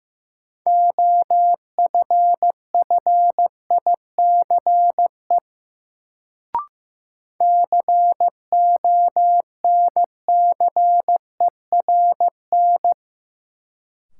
Morse code, heard as a keyed tone, about 15 wpm, 700 Hz. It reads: OFFICE CONCERN
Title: Sentences from Top 1000 Words - New Word Drill - 15wpm